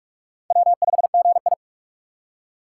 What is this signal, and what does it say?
Morse code, heard as a keyed tone, about 44 wpm, 700 Hz. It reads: W5GI
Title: USA Call Signs - 4 Characters - 6x Character Spacing - 45wpm